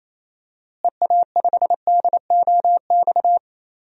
Morse code, heard as a keyed tone, about 28 wpm, 700 Hz. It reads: EA5DOX
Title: DXCC-100 Call Signs - 6 Characters - Repeated 2x - 28wpm